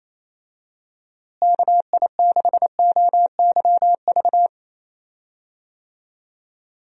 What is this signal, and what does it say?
Morse code, heard as a keyed tone, about 28 wpm, 700 Hz. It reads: KI6OYV
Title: USA Call Signs - 6 Characters - Repeated 2x - 28wpm